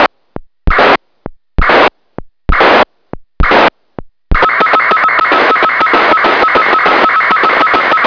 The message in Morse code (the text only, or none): none